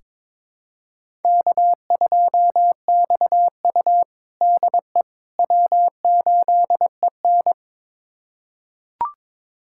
Morse code, heard as a keyed tone, about 22 wpm, 700 Hz. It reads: K2XU DE W8EN
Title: USA Callsign DE Callsign - 4 Characters - 22wpm